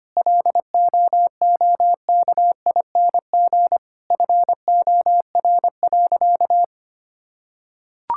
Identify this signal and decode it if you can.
Morse code, heard as a keyed tone, about 25 wpm, 700 Hz. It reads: LOOKING FOR.
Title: Sentences from Top 1000 Words - Review Part 6 - 25wpm